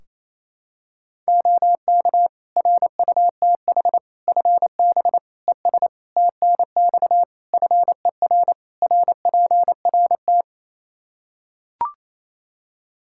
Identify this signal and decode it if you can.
Morse code, heard as a keyed tone, about 28 wpm, 700 Hz. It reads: OK RUTH FB ES TNX FER RPRT
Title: Practice QSO - Exchange 2 - Info - 28wpm